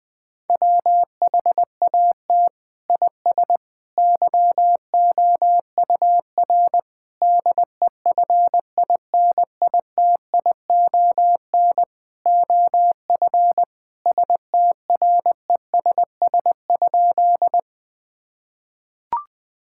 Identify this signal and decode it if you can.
Morse code, heard as a keyed tone, about 21 wpm, 700 Hz. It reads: WHAT IS YOUR DEFINITION OF STRESS?
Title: Sentences from Top 1000 Words - Review Part 6 - 20wpm